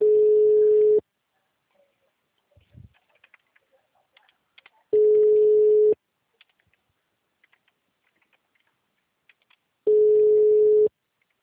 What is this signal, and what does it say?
Ringback tone (Europe, 425 Hz).
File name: out- -2007- - - .110